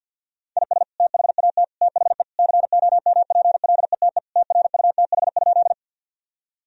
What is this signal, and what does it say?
Morse code, heard as a keyed tone, about 52 wpm, 700 Hz. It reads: IS THAT THE COMPLETE TRUTH?
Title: Sentences from Top 1000 Words - Review Part 2 - 50wpm